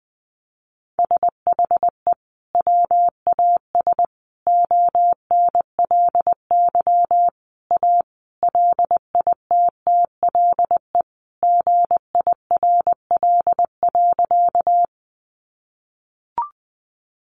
Morse code, heard as a keyed tone, about 20 wpm, 700 Hz. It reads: SHE WAS ONLY A LITTLE GIRL.
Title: Sentences from Top 400 Words 20wpm